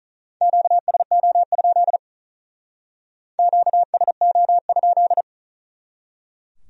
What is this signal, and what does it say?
Morse code, heard as a keyed tone, about 44 wpm, 700 Hz. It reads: QSO? QSO?
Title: Speed-Racing Q-Codes - 35wpm